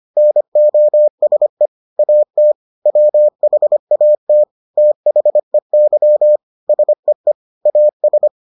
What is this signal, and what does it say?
Morse code, heard as a keyed tone, about 25 wpm, 600 Hz. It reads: NOSE AT WHAT THEY SEE AS